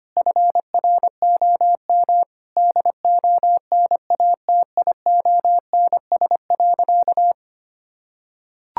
Morse code, heard as a keyed tone, about 25 wpm, 700 Hz. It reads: FROM DONATIONS.
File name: practice / 25wpm / Sentences with 9-Word Phrases - 25wpm